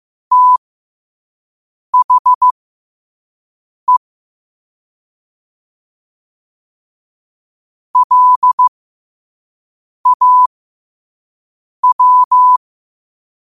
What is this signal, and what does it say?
Morse code, heard as a keyed tone, about 15 wpm, spaced as for 5 wpm, 1000 Hz. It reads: THE LAW
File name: qotdfiles / 5wpm